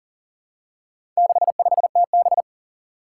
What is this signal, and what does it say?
Morse code, heard as a keyed tone, about 40 wpm, 700 Hz. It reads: 65TB
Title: Four Challenging Letter-Number Combinations 40wpm